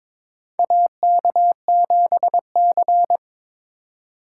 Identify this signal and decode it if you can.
Morse code, heard as a keyed tone, about 23 wpm, 700 Hz. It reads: AK7C
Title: USA Call Signs - 4 Characters - 2x Character Spacing - 22wpm